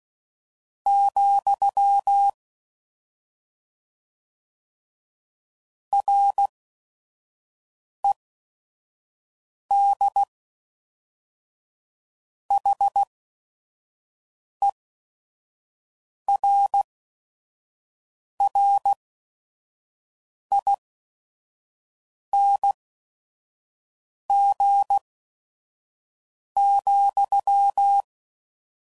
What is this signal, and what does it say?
Morse code, heard as a keyed tone, about 16 wpm, spaced as for 5 wpm, 800 Hz. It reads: , RED HERRING,